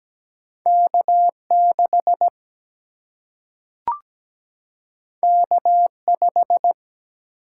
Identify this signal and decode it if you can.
Morse code, heard as a keyed tone, about 17 wpm, 700 Hz. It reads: K6 K5
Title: Single-Letter Single-Number 17wpm